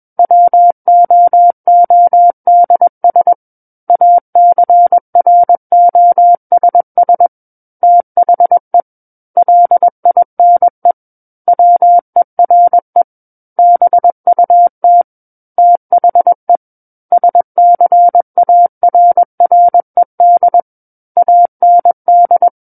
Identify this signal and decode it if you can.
Morse code, heard as a keyed tone, about 21 wpm, 700 Hz. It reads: WOODS ACROSS THE LINE WERE BUT THE SCARRED AND
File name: Book One / 12-Chapter 12 - 21 WPM